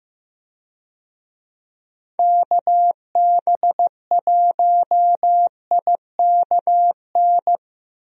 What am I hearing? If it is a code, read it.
Morse code, heard as a keyed tone, about 15 wpm, 700 Hz. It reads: KB1IKN
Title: USA Call Signs - 6 Characters - Repeated 2x - 15wpm